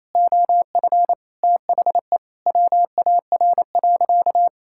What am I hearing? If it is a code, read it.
Morse code, heard as a keyed tone, about 28 wpm, 700 Hz. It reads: OF THE WAR.
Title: Sentences from Top 1000 Words - Review Part 6 - 28wpm